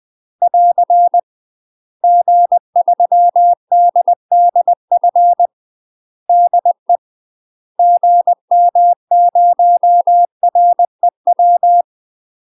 Morse code, heard as a keyed tone, about 20 wpm, 700 Hz. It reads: <AR> G3DDF DE GM0REW
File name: UK MASTER QSOs TRAINING / 20WPM / UK MASTER QSOs 0013